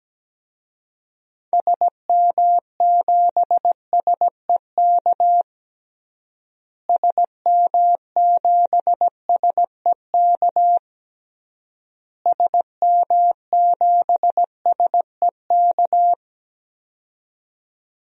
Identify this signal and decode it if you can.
Morse code, heard as a keyed tone, about 17 wpm, 700 Hz. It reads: SM7SEK SM7SEK SM7SEK
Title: DXCC-100 Call Signs - 6 Characters - Repeated 3x - 17wpm